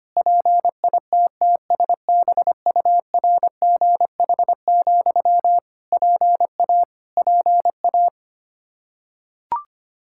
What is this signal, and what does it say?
Morse code, heard as a keyed tone, about 25 wpm, 700 Hz. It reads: PITTSBURGH, PA PA
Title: Practice QSO - Exchange 1 - Beginning - 25wpm